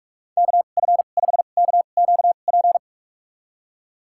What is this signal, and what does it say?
Morse code, heard as a keyed tone, about 46 wpm, spaced as for 33 wpm, 700 Hz. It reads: KF5KQP
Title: USA Call Signs - 2x Character Spacing - 45wpm